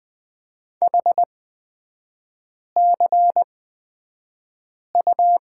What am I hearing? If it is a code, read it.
Morse code, heard as a keyed tone, about 20 wpm, spaced as for 5 wpm, 700 Hz. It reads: HCU
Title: 224 Lesson 36e - BT - Three Characters 20wpm